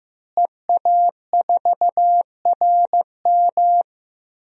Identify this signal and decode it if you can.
Morse code, heard as a keyed tone, about 16 wpm, 700 Hz. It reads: EA4RM
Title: DXCC-100 Call Signs - 5 Characters - Repeated 3x - 15wpm